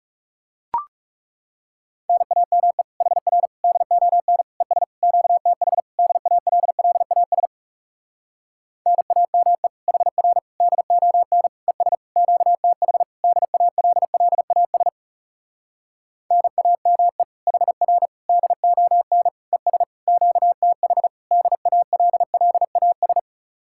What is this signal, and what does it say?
Morse code, heard as a keyed tone, about 40 wpm, 700 Hz. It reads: NAME HR DON ES QTH DALLAS NAME HR DON ES QTH DALLAS NAME HR DON ES QTH DALLAS